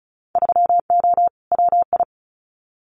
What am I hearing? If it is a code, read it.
Morse code, heard as a keyed tone, about 34 wpm, 700 Hz. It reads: 3O WI
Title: Winter Field Day Exchanges 35wpm